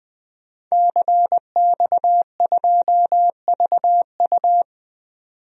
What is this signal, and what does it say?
Morse code, heard as a keyed tone, about 21 wpm, 700 Hz. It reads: CX2VU
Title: DXCC-100 Call Signs - 5 Characters - 20wpm